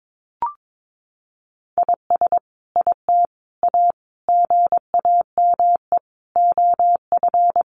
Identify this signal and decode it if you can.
Morse code, heard as a keyed tone, about 22 wpm, 700 Hz. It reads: IS IT A GAME OF
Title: Sentences from Top 1200 Words 22wpm